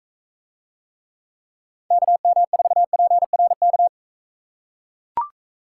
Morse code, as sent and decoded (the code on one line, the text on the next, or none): -.- -- ....- .--. .-. -.-
KM4PRK